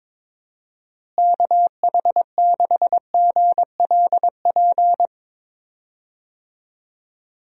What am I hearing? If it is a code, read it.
Morse code, heard as a keyed tone, about 22 wpm, 700 Hz. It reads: KH6GLP
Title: DXCC-100 Call Signs - 6 Characters - Repeated 2x - 22wpm